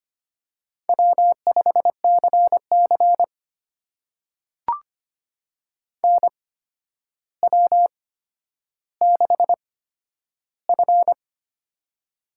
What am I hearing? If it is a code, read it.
Morse code, heard as a keyed tone, about 25 wpm, spaced as for 11 wpm, 700 Hz. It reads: W5CC NW6F
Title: USA Call Signs - 4 Characters - 8x Character Spacing - 25wpm